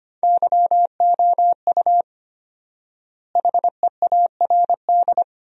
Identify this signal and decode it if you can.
Morse code, heard as a keyed tone, about 25 wpm, 700 Hz. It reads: YOU HEARD
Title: 4x Word-Spacing - Sentences from Top 500 Words - 25wpm